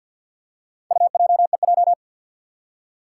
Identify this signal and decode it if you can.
Morse code, heard as a keyed tone, about 52 wpm, 700 Hz. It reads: UJEJ
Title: Four Challenging Letter-Number Combinations 50wpm